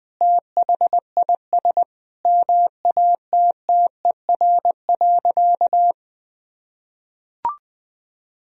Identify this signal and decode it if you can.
Morse code, heard as a keyed tone, about 20 wpm, 700 Hz. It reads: THIS MATTER.
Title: Sentences from Top 1000 Words - Review Part 2 - 20wpm